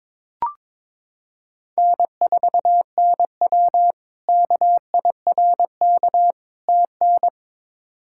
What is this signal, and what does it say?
Morse code, heard as a keyed tone, about 22 wpm, 700 Hz. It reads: N4NW KIRK TN
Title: K1USN SST QSO Practice - 22wpm